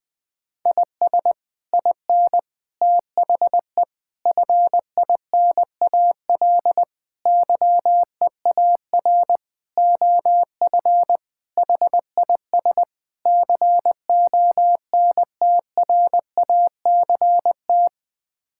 Morse code, heard as a keyed tone, about 21 wpm, 700 Hz. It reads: IS IN THE FINAL YEAR OF HIS CONTRACT
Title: Sets of 8 Words - 20wpm